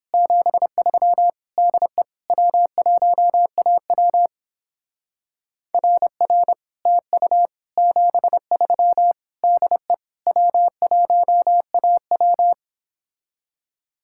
Morse code, heard as a keyed tone, about 29 wpm, 700 Hz. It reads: 73 DE W1AW RR TU 73 DE W1AW